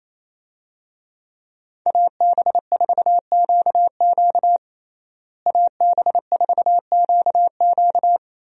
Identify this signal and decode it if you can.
Morse code, heard as a keyed tone, about 28 wpm, 700 Hz. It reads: AB4QQ AB4QQ
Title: USA Call Signs - 5 Characters - Repeated 2x - 28wpm